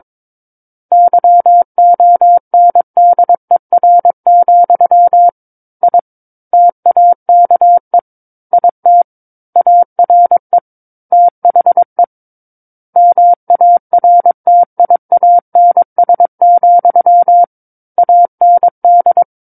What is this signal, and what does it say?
Morse code, heard as a keyed tone, about 22 wpm, 700 Hz. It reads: YONDER, I TAKE IT ARE THE MARTIANS, AND
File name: Book One / 13-Chapter 13 - 22 WPM